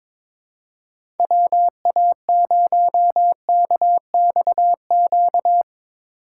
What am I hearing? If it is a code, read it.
Morse code, heard as a keyed tone, about 22 wpm, 700 Hz. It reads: WA0KXQ
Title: USA Call Signs - 6 Characters - 2x Character Spacing - 22wpm